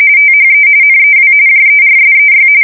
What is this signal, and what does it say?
RTTY, about 45.45 baud. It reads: THE NEW JERSEY
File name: fsk170 / battleshipnj